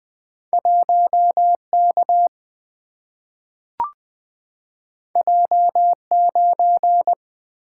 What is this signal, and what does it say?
Morse code, heard as a keyed tone, about 20 wpm, 700 Hz. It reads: 1K J9
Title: Single-Letter Single-Number 20wpm